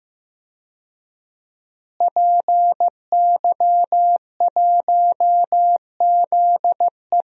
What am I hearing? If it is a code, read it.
Morse code, heard as a keyed tone, about 15 wpm, 700 Hz. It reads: PY1ZE
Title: DXCC-100 Call Signs - 5 Characters - Repeated 3x - 15wpm